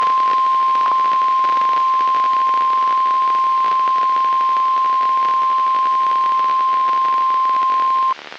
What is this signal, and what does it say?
Emergency Alert System (SAME) tones (the 1050 Hz attention tone). It is heard from 0.9 to 8.2 s.
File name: eas messages / EAS WXM95 Towanda